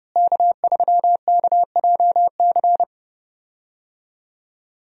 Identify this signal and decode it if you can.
Morse code, heard as a keyed tone, about 30 wpm, 700 Hz. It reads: K3KJC
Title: USA Call Signs - 5 Characters - Repeated 2x - 30wpm